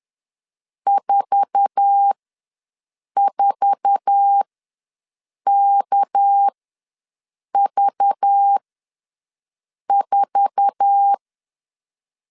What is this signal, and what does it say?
Morse code, heard as a keyed tone, about 10 wpm, spaced as for 6 wpm, 800 Hz. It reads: 44KV4
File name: LEZIONI / 8 YJ45 / 8 YJ4 ws 10wpm 4020